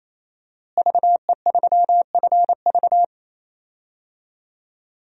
Morse code, heard as a keyed tone, about 27 wpm, 700 Hz. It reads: VE3FV
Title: DXCC-100 Call Signs - 5 Characters - Repeated 3x - 28wpm